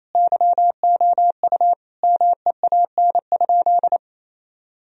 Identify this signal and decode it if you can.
Morse code, heard as a keyed tone, about 28 wpm, 700 Hz. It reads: YOU MEAN?
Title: Sentences from Top 1000 Words - Review Part 1 - 28wpm